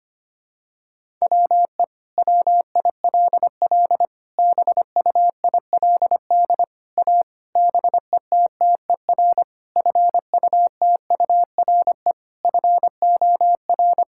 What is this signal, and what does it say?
Morse code, heard as a keyed tone, about 25 wpm, 700 Hz. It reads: WE WILL BUILD A BETTER FUTURE FOR
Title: Sentences from Top 1000 Words - Review Part 6 - 25wpm